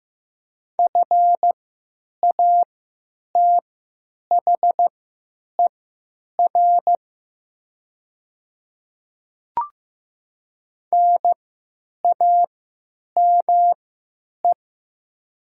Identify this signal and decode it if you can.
Morse code, heard as a keyed tone, about 15 wpm, spaced as for 9 wpm, 700 Hz. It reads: FATHER NAME
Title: 3x Character Spacing - Top 200 Words - 15wpm